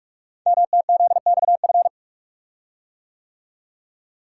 Morse code, heard as a keyed tone, about 44 wpm, 690 Hz. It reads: MTZXF
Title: Code Groups - 5 Letters - 45wpm